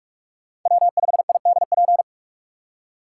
Morse code, heard as a keyed tone, about 41 wpm, 700 Hz. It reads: W5IDP